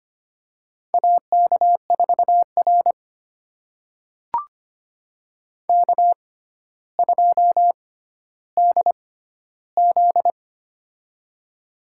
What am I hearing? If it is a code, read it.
Morse code, heard as a keyed tone, about 25 wpm, spaced as for 13 wpm, 700 Hz. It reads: AK4R K2DZ
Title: USA Call Signs - 4 Characters - 6x Character Spacing - 25wpm